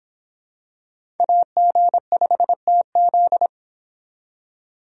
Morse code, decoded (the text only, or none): AG5TZ